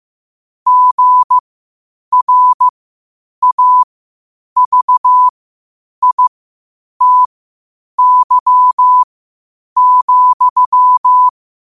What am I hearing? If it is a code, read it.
Morse code, heard as a keyed tone, about 15 wpm, spaced as for 8 wpm, 1000 Hz. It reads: GRAVITY,
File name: qotdfiles / 8wpm